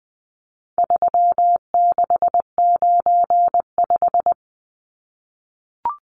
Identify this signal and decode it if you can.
Morse code, heard as a keyed tone, about 20 wpm, 700 Hz. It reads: 3695
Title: Numbers - 4-Digit 20wpm